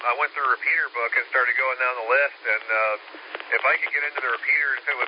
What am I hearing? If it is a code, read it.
Emergency Alert System (SAME) tones, heard as harsh buzzing bursts, with speech. It is heard from 0.6 to 4.1 s.